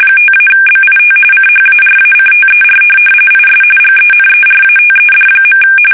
RTTY, about 45.45 baud. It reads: 5-BIT CHARACTER IS GENERATED.
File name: fsk850 / teletype-RTTY